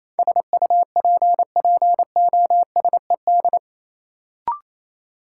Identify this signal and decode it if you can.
Morse code, heard as a keyed tone, about 28 wpm, 700 Hz. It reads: SUPPOSED
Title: Sets of 3 Words - New Word Drill - 28wpm